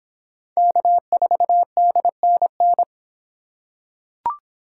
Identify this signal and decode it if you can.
Morse code, heard as a keyed tone, about 26 wpm, 700 Hz. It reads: K4DNN